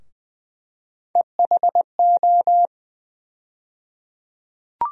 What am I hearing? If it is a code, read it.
Morse code, heard as a keyed tone, about 20 wpm, 700 Hz. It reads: EHO
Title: USA Call Sign Postfixes - 3 Letters 20wpm